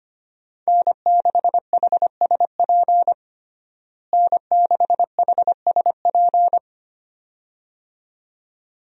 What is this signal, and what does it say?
Morse code, heard as a keyed tone, about 25 wpm, 700 Hz. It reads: N6HSP N6HSP
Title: USA Call Signs - 5 Characters - Repeated 2x - 25wpm